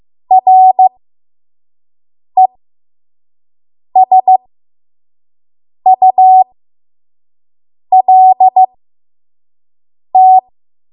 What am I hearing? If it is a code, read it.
Morse code, heard as a keyed tone, about 15 wpm, spaced as for 5 wpm, 750 Hz. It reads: RESULT
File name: Morse / 05WPM